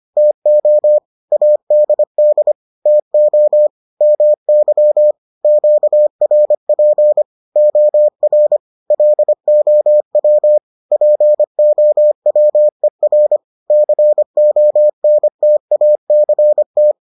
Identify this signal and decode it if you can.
Morse code, heard as a keyed tone, about 25 wpm, 600 Hz. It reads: TO ADD TO MY QRP OR LOW POWER CONTACT